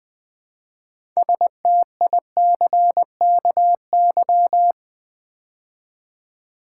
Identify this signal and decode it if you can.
Morse code, heard as a keyed tone, about 20 wpm, 700 Hz. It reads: STICKY
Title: Build A Word - 3 Letters At A Time - 6-Letter Words - 20wpm